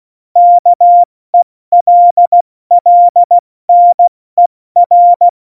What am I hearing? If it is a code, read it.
Morse code, heard as a keyed tone, about 16 wpm, spaced as for 14 wpm, 700 Hz. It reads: KELLNER
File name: Morsekurs / Witze / Witz 23. April 2019, 70 F80